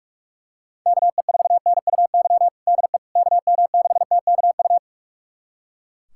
Morse code, heard as a keyed tone, about 42 wpm, 700 Hz. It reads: KE4NUY DE KM6TKU
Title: USA Callsign DE Callsign 45wpm